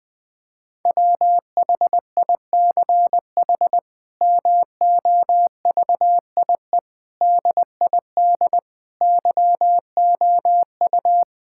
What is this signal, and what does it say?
Morse code, heard as a keyed tone, about 20 wpm, 700 Hz. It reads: WHICH MOVIE DID YOU
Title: Sentences from Top 1000 Words - Review Part 5 - 20wpm